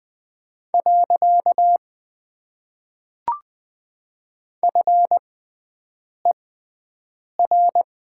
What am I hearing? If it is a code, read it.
Morse code, heard as a keyed tone, about 20 wpm, spaced as for 7 wpm, 700 Hz. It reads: . FER
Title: Top 90 QSO Elements - 6x Character Spacing - 20wpm